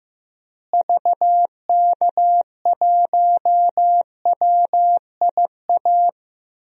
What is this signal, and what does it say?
Morse code, heard as a keyed tone, about 16 wpm, 700 Hz. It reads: VK1WIA
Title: DXCC-100 Call Signs - 6 Characters - Repeated 3x - 15wpm